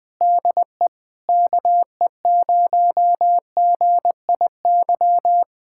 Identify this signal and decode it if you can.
Morse code, heard as a keyed tone, about 20 wpm, 700 Hz. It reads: DE KE0GIY
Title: Speed-Racing Common QSO Phrases - 17wpm